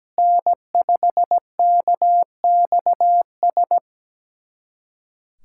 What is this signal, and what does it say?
Morse code, heard as a keyed tone, about 17 wpm, 700 Hz. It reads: N5KXS
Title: USA Callsign DE Callsign - 5 Characters - 17wpm